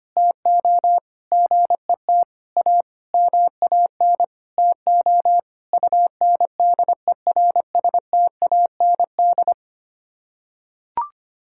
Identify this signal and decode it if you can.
Morse code, heard as a keyed tone, about 25 wpm, 700 Hz. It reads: TO GET A MAN TO UNDERSTAND
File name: Sets of 9 Words - 25wpm